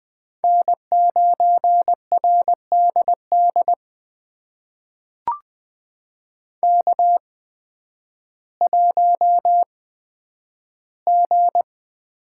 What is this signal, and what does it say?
Morse code, heard as a keyed tone, about 20 wpm, spaced as for 11 wpm, 700 Hz. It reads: N9RDD K1G
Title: USA Call Signs - 5 Characters - 8x Character Spacing - 20wpm